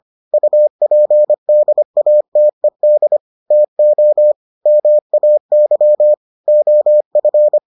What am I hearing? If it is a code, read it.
Morse code, heard as a keyed tone, about 25 wpm, 600 Hz. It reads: UPDATED TO MAY OF